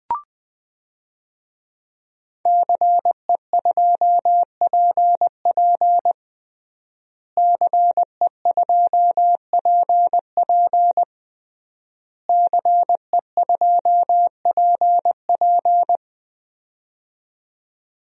Morse code, heard as a keyed tone, about 20 wpm, 700 Hz. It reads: CE2PP CE2PP CE2PP
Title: DXCC-100 Call Signs - 5 Characters - Repeated 3x - 20wpm